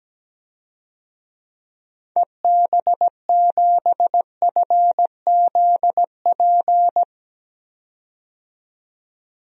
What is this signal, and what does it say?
Morse code, heard as a keyed tone, about 17 wpm, 700 Hz. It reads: EB7FZP
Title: DXCC-100 Call Signs - 6 Characters - 17wpm